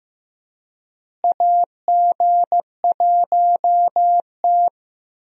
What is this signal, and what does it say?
Morse code, heard as a keyed tone, about 16 wpm, 700 Hz. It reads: AG1T
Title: USA Call Signs - 4 Characters - Repeated 3x - 15wpm